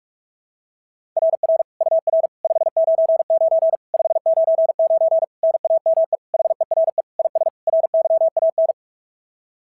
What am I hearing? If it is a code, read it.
Morse code, heard as a keyed tone, about 44 wpm, 690 Hz. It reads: RR UR 599 599 NAME HERE IS RYAN
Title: SOTA QSO Practice 45wpm